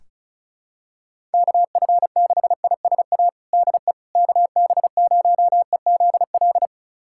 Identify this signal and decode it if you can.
Morse code, heard as a keyed tone, about 36 wpm, 700 Hz. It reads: KF6ISA DE KB0EZL
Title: USA Callsign DE Callsign - 6 Characters - 35wpm